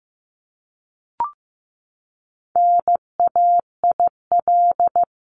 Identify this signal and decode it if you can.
Morse code, heard as a keyed tone, about 15 wpm, 700 Hz. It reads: NAIL
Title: Top 1501-1750 Words 15wpm